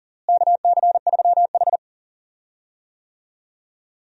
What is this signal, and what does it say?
Morse code, heard as a keyed tone, about 39 wpm, 700 Hz. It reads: KC3H
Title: USA Call Signs - 4 Characters - Repeated 2x - 40wpm